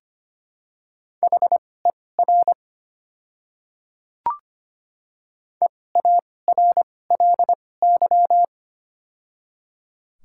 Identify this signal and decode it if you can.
Morse code, heard as a keyed tone, about 25 wpm, spaced as for 18 wpm, 700 Hz. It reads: HER EARLY
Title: 2x Character Spacing - Top 500 Words - 25wpm